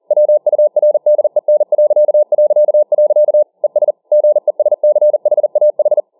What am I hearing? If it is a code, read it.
Morse code, heard as a keyed tone, about 40 wpm, 600 Hz. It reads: WURDEN... ES GESCHAH